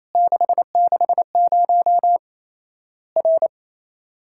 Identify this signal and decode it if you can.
Morse code, heard as a keyed tone, about 28 wpm, 700 Hz. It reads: 660 R
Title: Worked All Europe DX Contest - QTC Exchange - 28wpm